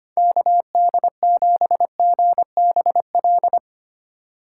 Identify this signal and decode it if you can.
Morse code, heard as a keyed tone, about 25 wpm, 690 Hz. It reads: KD7GBL
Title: USA Call Signs - 6 Characters - 2x Character Spacing - 25wpm